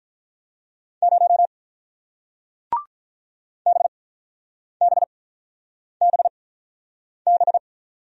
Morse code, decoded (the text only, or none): 0 BBBB